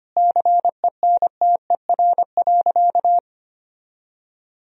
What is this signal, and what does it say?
Morse code, heard as a keyed tone, about 25 wpm, 700 Hz. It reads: CENTER.
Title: Sentences from Top 1000 Words - Review Part 2 - 25wpm